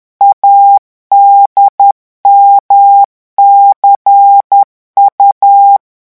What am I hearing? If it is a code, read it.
Morse code, heard as a keyed tone, about 10 wpm, 800 Hz. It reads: ADMCU
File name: morse mp3 / revision3-vitesse 10 mots